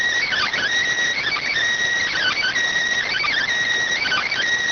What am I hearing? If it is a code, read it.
Emergency Alert System (SAME) tones, heard as harsh buzzing bursts. It is heard from 0.3 to 4.3 s.